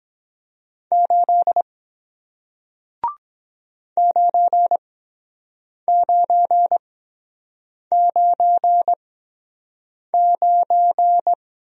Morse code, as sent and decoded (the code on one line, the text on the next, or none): ---.. / ----. ----. ----. ----.
8 9999